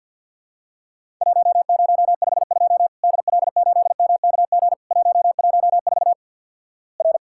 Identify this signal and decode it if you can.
Morse code, heard as a keyed tone, about 50 wpm, 700 Hz. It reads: 1052 DL8MKG 114 R
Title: Worked All Europe DX Contest - QTC Exchange with Repeats - 50wpm